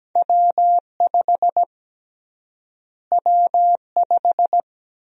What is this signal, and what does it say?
Morse code, heard as a keyed tone, about 17 wpm, 700 Hz. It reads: W5 W5
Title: USA Call Sign Prefixes - Repeated 2x - 17wpm